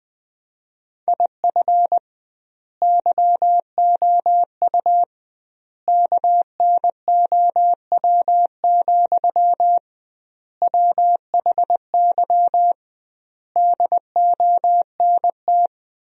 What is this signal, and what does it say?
Morse code, heard as a keyed tone, about 20 wpm, 700 Hz. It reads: IF YOU KNOW, WHY DONT
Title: 2x Word-Spacing - Sentences from Top 300 Words - 20wpm